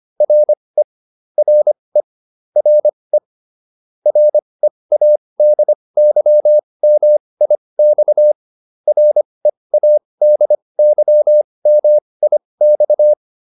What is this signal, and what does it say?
Morse code, heard as a keyed tone, about 25 wpm, spaced as for 20 wpm, 600 Hz. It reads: RE RE RE READYMIX READYMIX